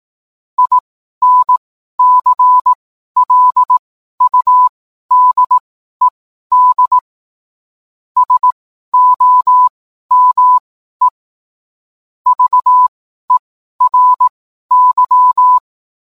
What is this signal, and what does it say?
Morse code, heard as a keyed tone, about 18 wpm, spaced as for 12 wpm, 1000 Hz. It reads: INCLUDED SOME VERY